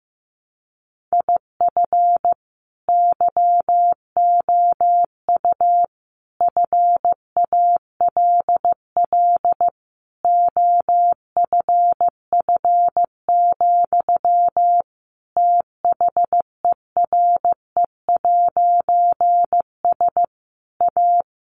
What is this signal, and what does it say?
Morse code, heard as a keyed tone, about 15 wpm, 700 Hz. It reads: IF YOU FALL OFF, THERE'S A
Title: A Sound Of Thunder 15wpm